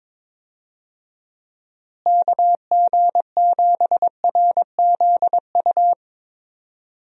Morse code, heard as a keyed tone, about 21 wpm, 700 Hz. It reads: KG7RZU
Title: USA Call Signs - 6 Characters - Repeated 3x - 22wpm